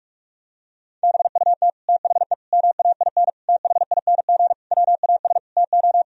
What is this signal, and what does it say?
Morse code, heard as a keyed tone, about 46 wpm, 700 Hz. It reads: BUT THE MAIN THING WAS TO
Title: Sentences from Top 1000 Words - Review Part 2 - 45wpm